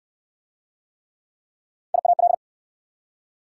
Morse code, heard as a keyed tone, about 66 wpm, 700 Hz. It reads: IAL